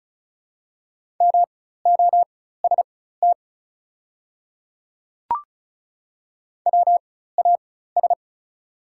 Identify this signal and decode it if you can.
Morse code, heard as a keyed tone, about 36 wpm, spaced as for 16 wpm, 690 Hz. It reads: MOST WAS
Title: 4x Character Spacing - Top 100 Words - 35wpm